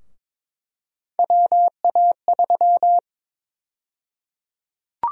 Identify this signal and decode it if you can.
Morse code, heard as a keyed tone, about 22 wpm, 700 Hz. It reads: WA3
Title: USA Call Sign Prefixes - 22wpm